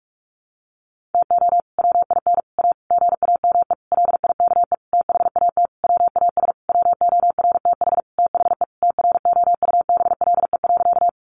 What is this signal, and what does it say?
Morse code, heard as a keyed tone, about 44 wpm, 700 Hz. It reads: TO WIN A GAME LIKE THAT WAS WORTH THE TROUBLE.
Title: Sentences from Top 1500 Words 45wpm